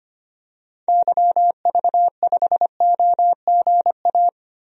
Morse code, heard as a keyed tone, about 25 wpm, 700 Hz. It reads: YV5OGA
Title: DXCC-100 Call Signs - 6 Characters - 25wpm